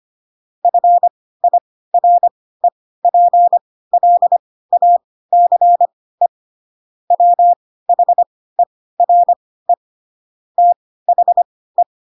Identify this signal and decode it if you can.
Morse code, heard as a keyed tone, about 25 wpm, spaced as for 16 wpm, 700 Hz. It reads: FIREPLACE WHERE THE